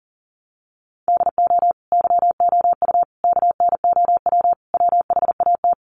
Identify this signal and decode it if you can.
Morse code, heard as a keyed tone, about 40 wpm, 700 Hz. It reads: DO YOU KNOW WHAT
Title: Sentences from Top 100 Words 40wpm